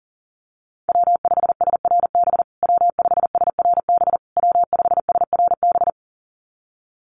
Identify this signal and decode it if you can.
Morse code, heard as a keyed tone, about 40 wpm, 700 Hz. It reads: W5SRB W5SRB W5SRB
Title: USA Call Signs Repeated 3x 40wpm